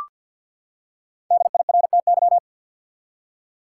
Morse code, heard as a keyed tone, about 53 wpm, 700 Hz. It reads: DIRTY
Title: Five Letter Words - 50wpm